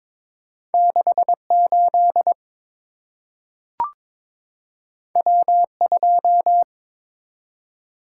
Morse code, heard as a keyed tone, about 22 wpm, 700 Hz. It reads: 68 W2
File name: Two Letter-Number Combinations - 22wpm